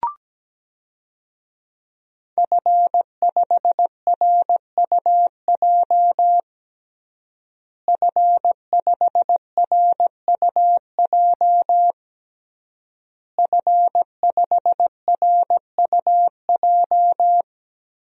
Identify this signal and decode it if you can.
Morse code, heard as a keyed tone, about 17 wpm, 700 Hz. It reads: F5RUJ F5RUJ F5RUJ
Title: DXCC-100 Call Signs - 5 Characters - Repeated 3x - 17wpm